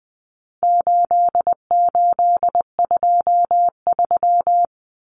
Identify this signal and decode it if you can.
Morse code, heard as a keyed tone, about 20 wpm, 700 Hz. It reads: 8823
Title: Numbers - 4-Digit 20wpm